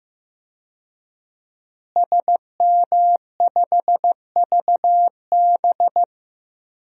Morse code, heard as a keyed tone, about 15 wpm, 700 Hz. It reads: SM5VB
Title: DXCC-100 Call Signs - 5 Characters - Repeated 3x - 15wpm